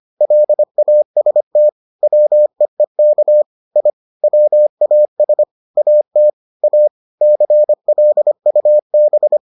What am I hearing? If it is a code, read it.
Morse code, heard as a keyed tone, about 25 wpm, 600 Hz. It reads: LAST WEEK I WAS AT A CLUB